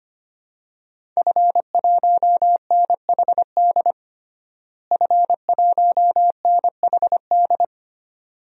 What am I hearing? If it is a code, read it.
Morse code, heard as a keyed tone, about 25 wpm, 700 Hz. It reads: F1NHD F1NHD
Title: DXCC-100 Call Signs - 5 Characters - Repeated 2x - 25wpm